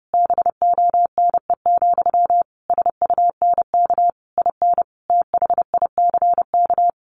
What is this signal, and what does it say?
Morse code, heard as a keyed tone, about 30 wpm, 700 Hz. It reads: BONE, SUNK IN THICK
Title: A Sound Of Thunder 30wpm